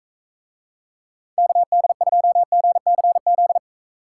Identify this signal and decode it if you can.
Morse code, heard as a keyed tone, about 41 wpm, 700 Hz. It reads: KD2GCZ